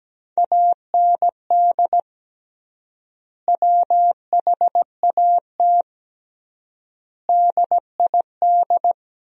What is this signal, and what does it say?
Morse code, heard as a keyed tone, about 17 wpm, 700 Hz. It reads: AND WHAT DID
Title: 3x Word-Spacing - Sentences from Top 100 Words - 17wpm